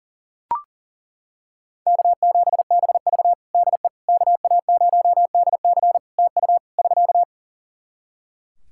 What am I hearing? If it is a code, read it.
Morse code, heard as a keyed tone, about 40 wpm, 700 Hz. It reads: K7BV DE KA0DC TU <SK>
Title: Practice QSO - Exchange 4 - Ending - 40wpm